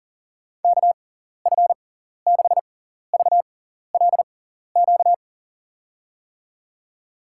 Morse code, heard as a keyed tone, about 40 wpm, spaced as for 14 wpm, 700 Hz. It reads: KF6VLQ
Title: USA Call Signs - 6x Character Spacing - 40wpm